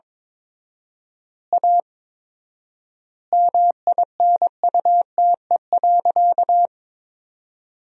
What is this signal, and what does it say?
Morse code, heard as a keyed tone, about 21 wpm, 700 Hz. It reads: A MINUTE.
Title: 4x Word-Spacing - Sentences from Top 500 Words - 22wpm